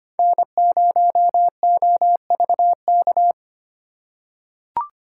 Morse code, heard as a keyed tone, about 25 wpm, 700 Hz. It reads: N0OVK
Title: USA Call Signs - 4x Character Spacing - 25wpm